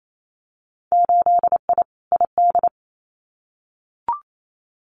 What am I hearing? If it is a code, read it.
Morse code, heard as a keyed tone, about 28 wpm, 690 Hz. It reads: 8I ID
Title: Winter Field Day Exchanges Encore 28wpm